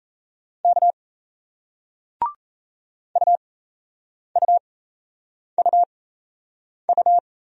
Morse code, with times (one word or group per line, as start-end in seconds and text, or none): 0.6-0.9 K
3.2-7.2 UUUU